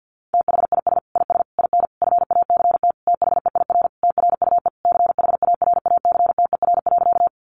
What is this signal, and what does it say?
Morse code, heard as a keyed tone, about 50 wpm, 700 Hz. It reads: THIS IS IN FACT THEIR TRUE CHARACTER.
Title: Sentences from Top 1200 Words 50wpm